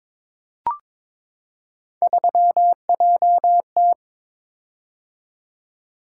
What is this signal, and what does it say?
Morse code, heard as a keyed tone, about 22 wpm, 700 Hz. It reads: 3JT
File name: practice / 22wpm / Three Letter-Number Combinations - 22wpm